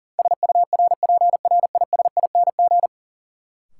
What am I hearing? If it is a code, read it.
Morse code, heard as a keyed tone, about 40 wpm, 700 Hz. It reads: SURPRISING
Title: Sets of 5 Words - New Word Drill - 40wpm